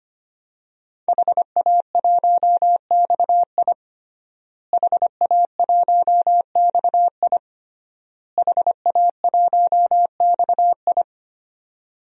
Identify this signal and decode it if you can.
Morse code, heard as a keyed tone, about 25 wpm, 700 Hz. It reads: HA1XI HA1XI HA1XI
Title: DXCC-100 Call Signs - 5 Characters - Repeated 3x - 25wpm